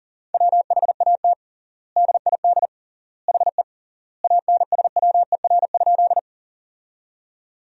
Morse code, heard as a keyed tone, about 40 wpm, 700 Hz. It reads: WHAT DID HE ANSWER?
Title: 3x Word-Spacing - Sentences from Top 300 Words - 40wpm